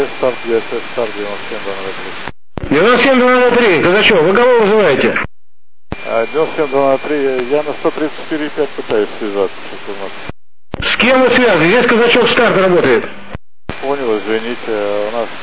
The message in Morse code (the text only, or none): none